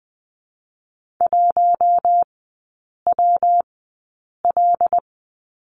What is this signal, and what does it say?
Morse code, heard as a keyed tone, about 20 wpm, 700 Hz. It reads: 1 W L
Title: 114 Lesson 19e - Letter w - Three Characters 20wpm